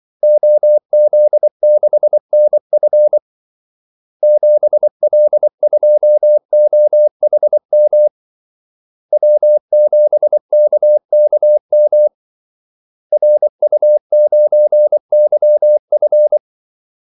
Morse code, heard as a keyed tone, about 24 wpm, 600 Hz. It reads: OZ6NF 7L2OHM W7KKM RU9YF
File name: Callsigns-24wpm